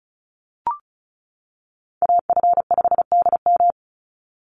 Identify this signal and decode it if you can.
Morse code, heard as a keyed tone, about 34 wpm, 700 Hz. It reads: AF5DM
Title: Call Signs - 35wpm